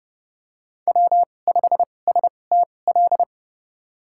Morse code, heard as a keyed tone, about 30 wpm, spaced as for 22 wpm, 700 Hz. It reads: W5STL
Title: USA Call Signs - 5 Characters - 2x Character Spacing - 30wpm